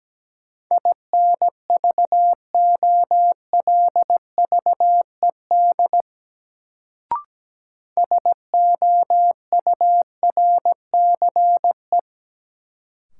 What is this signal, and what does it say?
Morse code, heard as a keyed tone, about 17 wpm, 700 Hz. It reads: INVOLVED SOURCE
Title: Sentences from Top 800 Words - New Word Drill - 17wpm